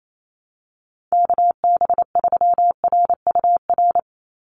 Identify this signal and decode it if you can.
Morse code, heard as a keyed tone, about 28 wpm, 700 Hz. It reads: KB3RUR
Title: USA Call Signs 28wpm